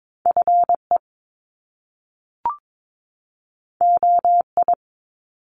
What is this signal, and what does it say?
Morse code, heard as a keyed tone, about 22 wpm, 700 Hz. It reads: FE OI
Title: Top Two Letter Combinations in Words 22wpm